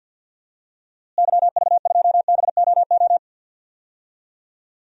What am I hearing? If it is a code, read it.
Morse code, heard as a keyed tone, about 51 wpm, 700 Hz. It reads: YV2BOO
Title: DXCC-100 Call Signs - 6 Characters - Repeated 2x - 50wpm